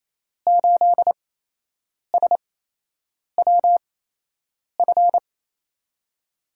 Morse code, heard as a keyed tone, about 28 wpm, spaced as for 8 wpm, 700 Hz. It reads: 8SWF
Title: USA Call Signs - 5 Characters - 8x Character Spacing - 28wpm